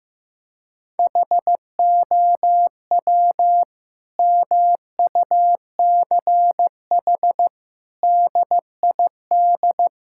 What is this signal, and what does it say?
Morse code, heard as a keyed tone, about 16 wpm, 700 Hz. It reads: HOW MUCH DID
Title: Sentences from Top 1000 Words - Review Part 5 - 15wpm